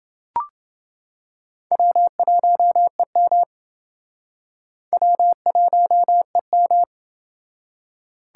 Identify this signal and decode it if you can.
Morse code, heard as a keyed tone, about 29 wpm, 700 Hz. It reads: W1EM W1EM